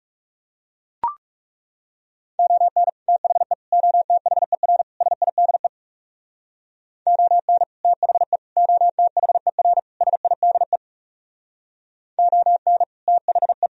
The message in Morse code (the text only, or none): ON THE OTHER SIDE ON THE OTHER SIDE ON THE